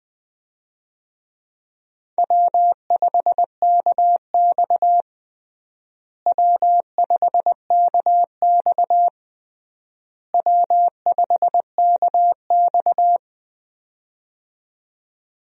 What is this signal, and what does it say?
Morse code, heard as a keyed tone, about 20 wpm, 700 Hz. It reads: W5KX W5KX W5KX
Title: USA Call Signs - 4 Characters - Repeated 3x - 20wpm